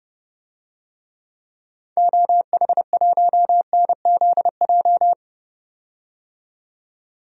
Morse code, heard as a keyed tone, about 31 wpm, 700 Hz. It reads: OH1NZJ
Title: DXCC-100 Call Signs - 6 Characters - 30wpm